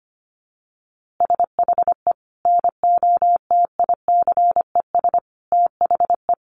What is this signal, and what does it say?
Morse code, heard as a keyed tone, about 25 wpm, 700 Hz. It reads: SHE NOTICES THE
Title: Easy Phrases 25wpm